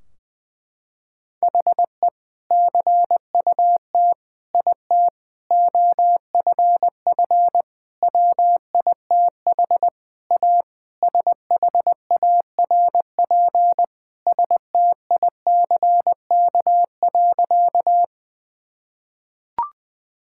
Morse code, heard as a keyed tone, about 19 wpm, 700 Hz. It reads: HE CUT IT OFF WITH A SHARP STICK.
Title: Sentences from Top 1600 Words 20wpm